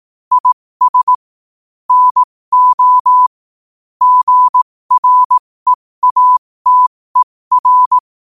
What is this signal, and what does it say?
Morse code, heard as a keyed tone, about 18 wpm, spaced as for 15 wpm, 1000 Hz. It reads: IS NO GREATER
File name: qotdfiles / 15wpm